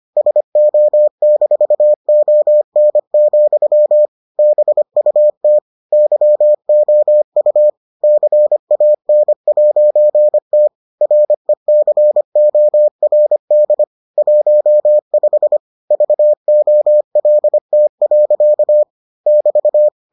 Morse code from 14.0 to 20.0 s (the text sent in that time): 15 VOLT. <BT>